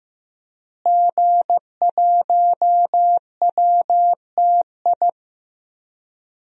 Morse code, heard as a keyed tone, about 15 wpm, 700 Hz. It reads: G1WTI
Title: DXCC-100 Call Signs - 5 Characters - Repeated 2x - 15wpm